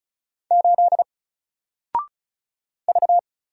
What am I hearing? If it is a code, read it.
Morse code, heard as a keyed tone, about 35 wpm, 700 Hz. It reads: 8 V
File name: Single-Letter-Number-35wpm